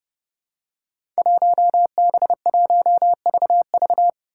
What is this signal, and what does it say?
Morse code, heard as a keyed tone, about 30 wpm, 700 Hz. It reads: 1B1VV
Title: Five Challenging Letter-Number Combinations 30wpm